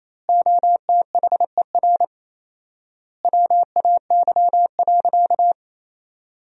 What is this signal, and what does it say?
Morse code, heard as a keyed tone, about 28 wpm, 700 Hz. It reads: OTHER WAY.
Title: 4x Word-Spacing - Sentences from Top 100 Words - 28wpm